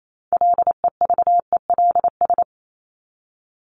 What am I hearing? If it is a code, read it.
Morse code, heard as a keyed tone, about 28 wpm, 700 Hz. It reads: LEVELS
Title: ES:S Suffix - Top 100 Words with Base Words 28wpm